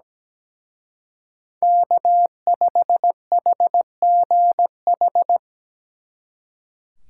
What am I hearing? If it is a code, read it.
Morse code, heard as a keyed tone, about 20 wpm, 700 Hz. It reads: K5HGH
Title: Speed Racing - USA Call Signs 17wpm